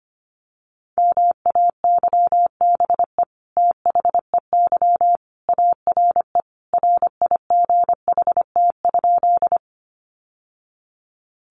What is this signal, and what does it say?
Morse code, heard as a keyed tone, about 25 wpm, 700 Hz. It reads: MAYBE THEY ARE RIGHT?
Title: Sentences from Top 1000 Words 25wpm